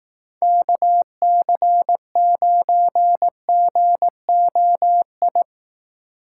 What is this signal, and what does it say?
Morse code, heard as a keyed tone, about 19 wpm, 700 Hz. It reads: KC9GOI
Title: Speed Racing - USA 2x3 Call Signs 15wpm